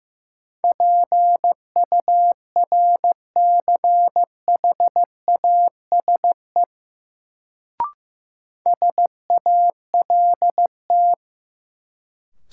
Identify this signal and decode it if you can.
Morse code, heard as a keyed tone, about 15 wpm, 700 Hz. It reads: PURCHASE SALT
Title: Sentences from Top 800 Words - New Word Drill - 15wpm